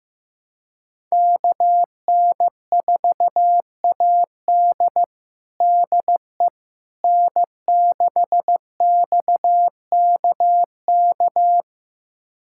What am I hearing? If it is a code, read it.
Morse code, heard as a keyed tone, about 15 wpm, 700 Hz. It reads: KN4AD DE N6XKK
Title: USA Callsign DE Callsign - 5 Characters - 15wpm